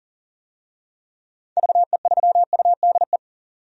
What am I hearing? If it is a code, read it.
Morse code, heard as a keyed tone, about 40 wpm, 700 Hz. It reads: VE3UDE
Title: DXCC-100 Call Signs - 6 Characters - 40wpm